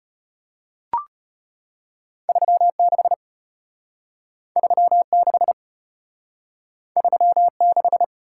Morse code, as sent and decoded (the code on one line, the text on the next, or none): ...-- / -.... / ...-- / -.... / ...-- / -....
3 6 3 6 3 6